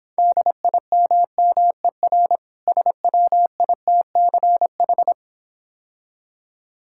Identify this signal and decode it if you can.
Morse code, heard as a keyed tone, about 26 wpm, 690 Hz. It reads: DIMMER SWITCH